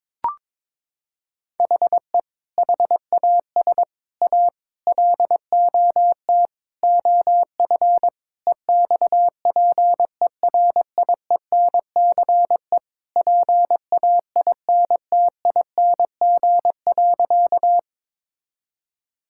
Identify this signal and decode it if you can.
Morse code, heard as a keyed tone, about 22 wpm, 700 Hz. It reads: HE HAS A LOT OF EXPERIENCE PAINTING.
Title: Sentences from Top 1000 Words - Review Part 6 - 22wpm